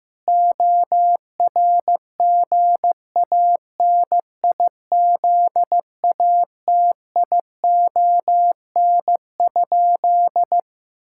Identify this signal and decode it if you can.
Morse code, heard as a keyed tone, about 15 wpm, 700 Hz. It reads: ORGANIZATION?
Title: Sentences from Top 1000 Words - Review Part 5 - 15wpm